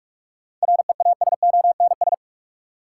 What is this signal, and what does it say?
Morse code, heard as a keyed tone, about 46 wpm, 700 Hz. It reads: REASONS
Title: Sets of 4 Words - New Word Drill - 45wpm